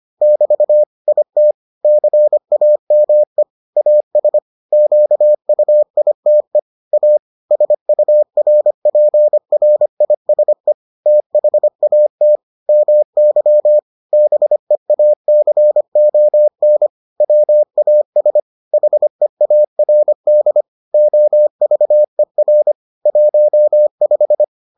Morse code, heard as a keyed tone, about 25 wpm, 600 Hz. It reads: <BT> IT CAME AS QUITE A SURPRISE THAT MY BEACON WAS HEARD OVER 15